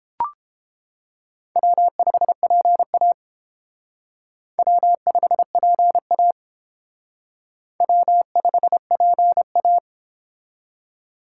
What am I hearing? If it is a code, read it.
Morse code, heard as a keyed tone, about 28 wpm, 700 Hz. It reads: W5PA W5PA W5PA